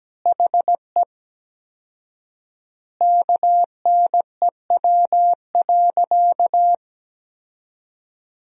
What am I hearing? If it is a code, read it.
Morse code, heard as a keyed tone, about 17 wpm, 700 Hz. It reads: HE KNEW.
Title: 4x Word-Spacing - Sentences from Top 400 Words - 17wpm